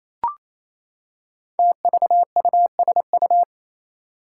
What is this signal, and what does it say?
Morse code, heard as a keyed tone, about 27 wpm, 700 Hz. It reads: TVUSU
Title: Five Challenging Letter-Number Combinations 28wpm